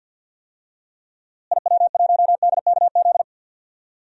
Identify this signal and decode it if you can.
Morse code, heard as a keyed tone, about 51 wpm, 700 Hz. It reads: IW1DKZ
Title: DXCC-100 Call Signs - 6 Characters - Repeated 3x - 50wpm